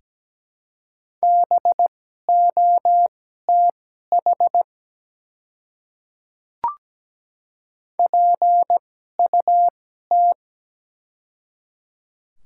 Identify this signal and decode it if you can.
Morse code, heard as a keyed tone, about 17 wpm, spaced as for 12 wpm, 700 Hz. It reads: BOTH PUT
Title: 2x Character Spacing - Top 300 Words - 17wpm